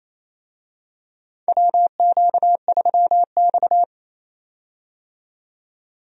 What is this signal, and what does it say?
Morse code, heard as a keyed tone, about 28 wpm, 700 Hz. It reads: WQ3X
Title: USA Call Signs - 4 Characters - 28wpm